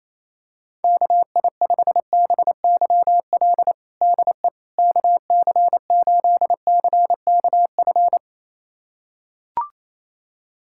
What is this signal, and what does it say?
Morse code, heard as a keyed tone, about 28 wpm, 700 Hz. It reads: KI5BYL DE KC8CKF
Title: USA Callsign DE Callsign - 6 Characters - 28wpm